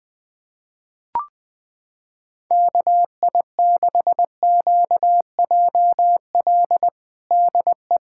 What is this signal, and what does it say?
Morse code, heard as a keyed tone, about 20 wpm, 700 Hz. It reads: KI6QJL DE
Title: USA Callsign DE Callsign - 6 Characters - 20wpm